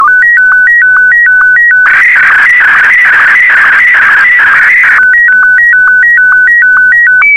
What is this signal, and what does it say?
SSTV (slow-scan television), Martin 1.